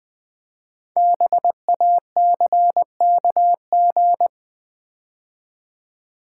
Morse code, heard as a keyed tone, about 20 wpm, 700 Hz. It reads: BACKG
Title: Build A Word - 5 Letters At A Time - 10-Letter Words - 20wpm